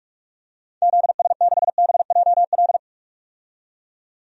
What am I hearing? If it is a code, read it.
Morse code, heard as a keyed tone, about 46 wpm, 700 Hz. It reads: ZS6BJL
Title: DXCC-100 Call Signs - 6 Characters - 45wpm